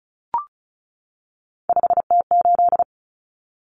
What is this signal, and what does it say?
Morse code, heard as a keyed tone, about 34 wpm, 700 Hz. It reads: 5T8
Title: Common Cut Numbers - 3-Digit 35wpm